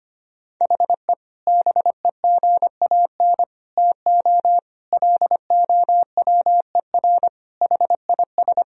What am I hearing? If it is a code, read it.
Morse code, heard as a keyed tone, about 25 wpm, 700 Hz. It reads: HE BEGAN TO LOWER HIS
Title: Sentences from Top 1000 Words - Review Part 4 - 25wpm